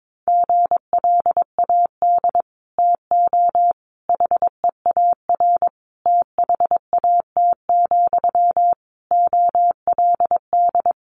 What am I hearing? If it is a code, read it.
Morse code, heard as a keyed tone, about 22 wpm, 700 Hz. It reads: GLAD TO HEAR THAT, OLD
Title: Sentences from Top 800 Words 22wpm